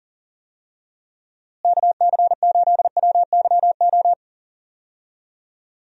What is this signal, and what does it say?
Morse code, heard as a keyed tone, about 38 wpm, 700 Hz. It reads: KC8WYO
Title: USA Call Signs - 6 Characters - Repeated 2x - 40wpm